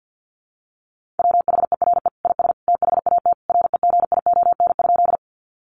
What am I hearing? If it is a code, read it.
Morse code, heard as a keyed tone, about 50 wpm, 700 Hz. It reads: WHERE IS THAT REGION?
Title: Sentences from Top 1500 Words 50wpm